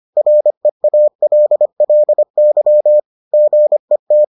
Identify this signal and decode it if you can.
Morse code, heard as a keyed tone, about 25 wpm, 600 Hz. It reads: REALLY GET